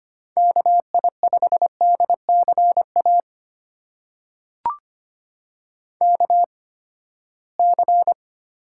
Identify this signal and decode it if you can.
Morse code, heard as a keyed tone, about 25 wpm, spaced as for 17 wpm, 700 Hz. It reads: KI5DCA KC
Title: USA Call Signs - 8x Character Spacing - 25wpm